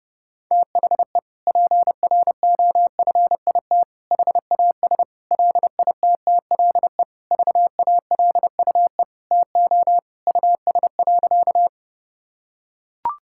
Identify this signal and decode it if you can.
Morse code, heard as a keyed tone, about 30 wpm, 700 Hz. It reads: THE PROFIT HAS LITTLE VALUE TO US.
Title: Sentences from Top 1000 Words - Review Part 5 - 30wpm